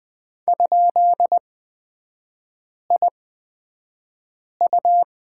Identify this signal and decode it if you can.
Morse code, heard as a keyed tone, about 20 wpm, spaced as for 5 wpm, 700 Hz. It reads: ?IU
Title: 229 Lesson 37e - Letter x - Three Characters 20wpm